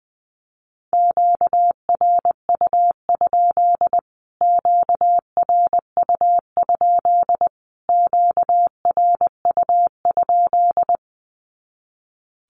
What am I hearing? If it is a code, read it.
Morse code, heard as a keyed tone, about 20 wpm, 700 Hz. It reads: QRU? QRU? QRU?
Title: Q-Codes Repeated 3x 20wpm